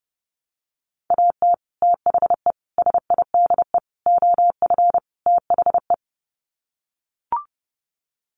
Morse code, heard as a keyed tone, about 30 wpm, 700 Hz. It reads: AT THE SIDE OF THE
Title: Sets of 5 Words Encore2 30wpm